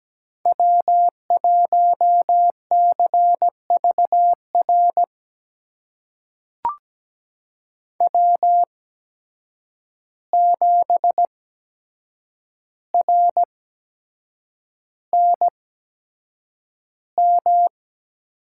Morse code, heard as a keyed tone, about 17 wpm, spaced as for 7 wpm, 700 Hz. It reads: W1CVR W7RNM
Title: USA Call Signs - 5 Characters - 8x Character Spacing - 17wpm